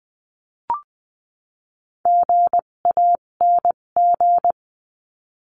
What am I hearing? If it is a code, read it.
Morse code, heard as a keyed tone, about 20 wpm, spaced as for 17 wpm, 700 Hz. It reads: GANG
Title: 257 Lesson 42b - Words 14-20wpm